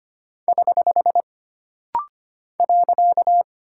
Morse code, heard as a keyed tone, about 25 wpm, 700 Hz. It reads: <HH> .
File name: Single Punctuation-Prosign - Rapid-Fire - 25wpm